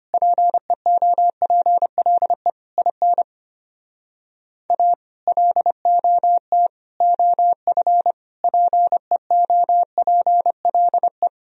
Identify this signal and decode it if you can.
Morse code, heard as a keyed tone, about 33 wpm, 700 Hz. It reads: PEOPLE IN A LOT OF PEOPLE